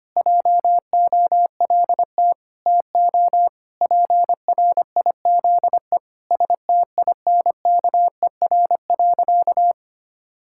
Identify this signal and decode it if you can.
Morse code, heard as a keyed tone, about 25 wpm, 700 Hz. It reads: JOLT TO PRIZE STINKER.
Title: Pangram Challenge v2 - 25wpm